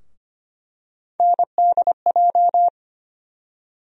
Morse code, heard as a keyed tone, about 25 wpm, 700 Hz. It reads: NDJ
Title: USA Call Sign Postfixes - 3 Letters 25wpm